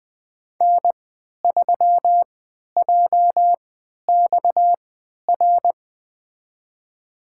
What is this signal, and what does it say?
Morse code, heard as a keyed tone, about 20 wpm, spaced as for 11 wpm, 700 Hz. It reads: N3JXR
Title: USA Call Signs - 3x Character Spacing - 20wpm